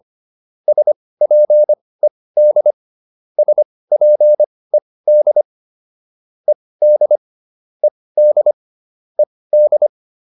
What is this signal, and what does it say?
Morse code, heard as a keyed tone, about 25 wpm, spaced as for 18 wpm, 600 Hz. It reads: SPED SPED ED ED ED